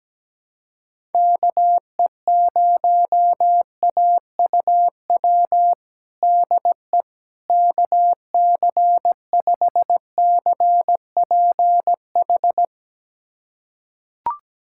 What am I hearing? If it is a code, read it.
Morse code, heard as a keyed tone, about 17 wpm, 700 Hz. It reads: KE0AUW DE KC5CPH
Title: USA Callsign DE Callsign - 6 Characters - 17wpm